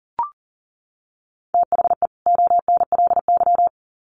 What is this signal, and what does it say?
Morse code, heard as a keyed tone, about 40 wpm, 700 Hz. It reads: THE ONLY
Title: Sets of 2 Words 40wpm